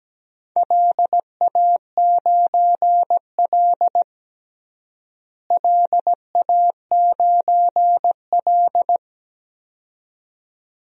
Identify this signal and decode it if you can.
Morse code, heard as a keyed tone, about 17 wpm, 700 Hz. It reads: LA9L LA9L
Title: DXCC-100 Call Signs - 4 Characters - Repeated 2x - 17wpm